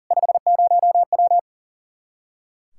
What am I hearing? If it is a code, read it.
Morse code, heard as a keyed tone, about 40 wpm, 700 Hz. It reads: 50W
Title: K1USN SST QSO Practice 40wpm